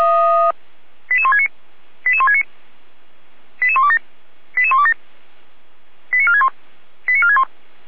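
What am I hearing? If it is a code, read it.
Five-tone selective call (ZVEI), heard as quick runs of five tones, tones of about 70 ms. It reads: 88169 88169 88127 88127 89461 89461
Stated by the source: Received by SDR, ZVEI five-tone call 88367.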